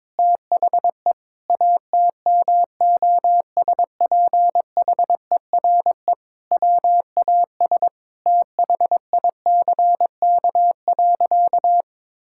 Morse code, heard as a keyed tone, about 21 wpm, 700 Hz. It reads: THE ATMOSPHERE WAS THICK.
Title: Sentences from Top 1600 Words 22wpm